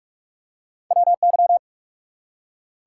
Morse code, heard as a keyed tone, about 44 wpm, 700 Hz. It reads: WY
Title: US States and Canadian Provinces 45wpm